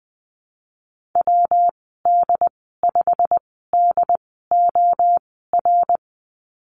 Morse code, heard as a keyed tone, about 20 wpm, spaced as for 15 wpm, 700 Hz. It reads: WD5DOR
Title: 167 Lesson 27g - Number 7 - Call Signs 20wpm